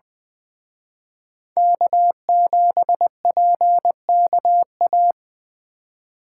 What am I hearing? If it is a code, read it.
Morse code, heard as a keyed tone, about 23 wpm, 700 Hz. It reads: K7PKA
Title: Speed Racing - USA 1x3 Call Signs 20wpm